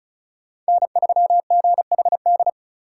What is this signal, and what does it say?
Morse code, heard as a keyed tone, about 36 wpm, 700 Hz. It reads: N3GHD
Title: USA Call Signs - 5 Characters - 4x Character Spacing - 35wpm